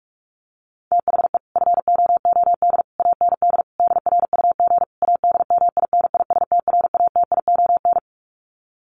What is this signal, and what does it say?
Morse code, heard as a keyed tone, about 44 wpm, 700 Hz. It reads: THE FOOD AND DRUG ADMINISTRATION
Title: Sets of 5 Words 45wpm